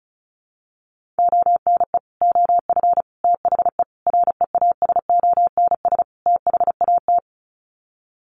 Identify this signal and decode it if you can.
Morse code, heard as a keyed tone, about 34 wpm, 700 Hz. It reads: ONE OF THE REASONS THAT
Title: Sets of 5 Words Encore 35wpm